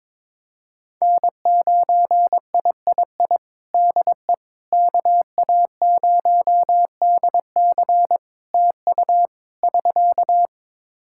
Morse code, heard as a keyed tone, about 22 wpm, 700 Hz. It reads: N9III DE KA0DC TU <SK>
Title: Practice QSO - Exchange 4 - Ending - 22wpm